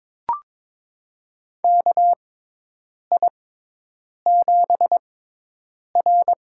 Morse code, heard as a keyed tone, about 22 wpm, spaced as for 8 wpm, 700 Hz. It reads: KI7R
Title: USA Call Signs - 6x Character Spacing - 22wpm